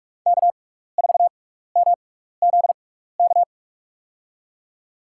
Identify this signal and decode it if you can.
Morse code, heard as a keyed tone, about 44 wpm, spaced as for 15 wpm, 700 Hz. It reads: K4MZK
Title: USA Call Signs - 6x Character Spacing - 45wpm